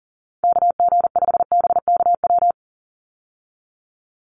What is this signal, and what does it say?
Morse code, heard as a keyed tone, about 38 wpm, 700 Hz. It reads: KG5BKW
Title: USA Call Signs 40wpm